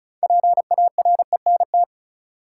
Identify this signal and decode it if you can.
Morse code, heard as a keyed tone, about 34 wpm, 700 Hz. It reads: PARENT
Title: Six Letter Words - 35wpm